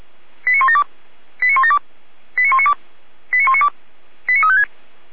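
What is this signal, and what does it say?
Five-tone selective call (ZVEI), heard as quick runs of five tones, tones of about 70 ms. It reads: 89172 89172 89192 89192 89356
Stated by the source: ZVEI five-tone selective call received by SDR.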